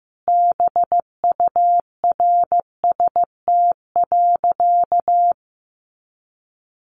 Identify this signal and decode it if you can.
Morse code, heard as a keyed tone, about 15 wpm, 700 Hz. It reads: BURST.
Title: A Sound Of Thunder 15wpm